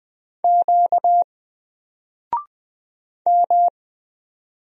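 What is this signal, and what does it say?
Morse code, heard as a keyed tone, about 20 wpm, 700 Hz. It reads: Q M
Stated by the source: Single letter and number Morse drill at 20 wpm, from N4LCD.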